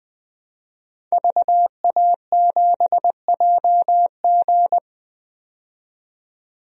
Morse code, heard as a keyed tone, about 20 wpm, 700 Hz. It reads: VA7JG
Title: DXCC-100 Call Signs - 5 Characters - 20wpm